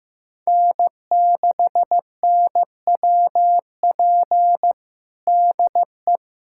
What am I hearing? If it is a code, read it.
Morse code, heard as a keyed tone, about 15 wpm, 700 Hz. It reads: N6NWP DE
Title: USA Callsign DE Callsign - 5 Characters - 15wpm